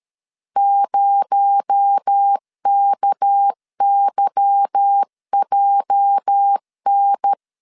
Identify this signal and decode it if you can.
Morse code, heard as a keyed tone, about 13 wpm, 800 Hz. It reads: 0KYJN
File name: LEZIONI / 8 YJ45 / 8 YJ 12wpm 2235